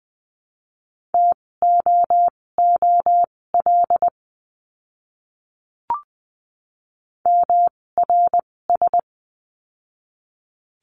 Morse code, heard as a keyed tone, about 20 wpm, spaced as for 16 wpm, 700 Hz. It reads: TOOL MRS
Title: 256 Lesson 42a - Words 12-20wpm